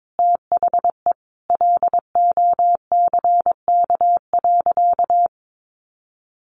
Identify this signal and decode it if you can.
Morse code, heard as a keyed tone, about 22 wpm, 700 Hz. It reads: THE LOCK.
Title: Sentences from Top 1300 Words 22wpm